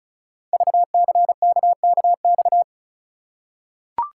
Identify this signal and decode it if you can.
Morse code, heard as a keyed tone, about 36 wpm, 700 Hz. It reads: VCKKX
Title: Code Groups - 5 Letters - 35wpm